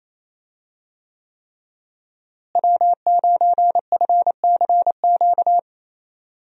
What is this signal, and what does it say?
Morse code, heard as a keyed tone, about 28 wpm, 700 Hz. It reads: W9FCQ
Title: USA Call Signs - 5 Characters - Repeated 3x - 28wpm